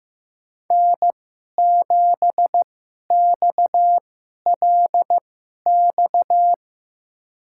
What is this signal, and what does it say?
Morse code, heard as a keyed tone, about 15 wpm, spaced as for 11 wpm, 700 Hz. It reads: N7XLX
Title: USA Call Signs - 2x Character Spacing - 15wpm